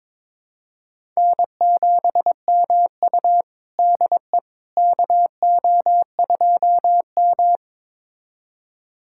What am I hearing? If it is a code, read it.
Morse code, heard as a keyed tone, about 22 wpm, 700 Hz. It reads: N7MU DE KO2M
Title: USA Callsign DE Callsign - 4 Characters - 22wpm